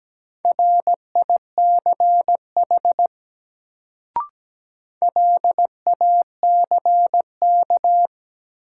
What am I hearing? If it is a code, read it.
Morse code, heard as a keyed tone, about 17 wpm, 700 Hz. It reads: RICH LACK
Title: Sentences from Top 1100 Words - New Word Drill - 17wpm